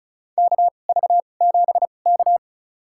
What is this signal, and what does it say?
Morse code, heard as a keyed tone, about 36 wpm, spaced as for 25 wpm, 700 Hz. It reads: KV7K
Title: USA Call Signs - 2x Character Spacing - 35wpm